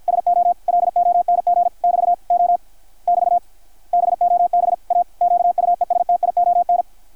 Morse code, heard as a keyed tone, about 53 wpm, 700 Hz. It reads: SO LONG <BT> O <BT> BOB A QUESTION
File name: archived / CRSnet